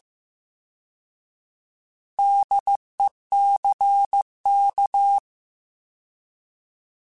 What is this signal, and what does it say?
Morse code, heard as a keyed tone, about 15 wpm, 800 Hz. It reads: DECK
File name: Alphabetical / Alphabetical mp3s / Ds @13